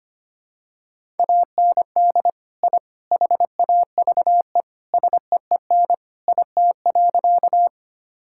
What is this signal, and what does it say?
Morse code, heard as a keyed tone, about 25 wpm, 700 Hz. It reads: AND I HAVE SEEN IT.
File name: practice / 25wpm / Sentences with 3-Word Phrases - 25wpm